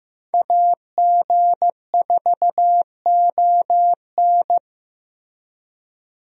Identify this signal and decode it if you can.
Morse code, heard as a keyed tone, about 15 wpm, 700 Hz. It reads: AG4ON
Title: USA Call Signs - 5 Characters - Repeated 2x - 15wpm